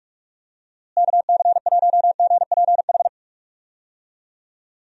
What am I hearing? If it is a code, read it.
Morse code, heard as a keyed tone, about 46 wpm, 700 Hz. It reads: KC1GPH
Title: USA Call Signs - 6 Characters - Repeated 3x - 45wpm